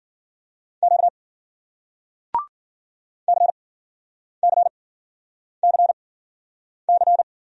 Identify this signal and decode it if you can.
Morse code, heard as a keyed tone, about 49 wpm, spaced as for 9 wpm, 700 Hz. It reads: Q CCCC
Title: Speed-Racing Single Letter-Number 40wpm